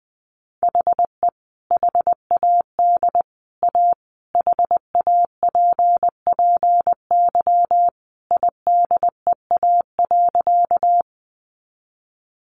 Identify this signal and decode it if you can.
Morse code, heard as a keyed tone, about 20 wpm, 700 Hz. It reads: HE HAD A HAPPY IDEA.
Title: Sentences from Top 1000 Words 20wpm